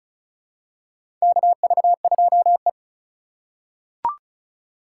Morse code, heard as a keyed tone, about 34 wpm, 700 Hz. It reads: KV2E
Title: USA Call Signs - 3x Character Spacing - 35wpm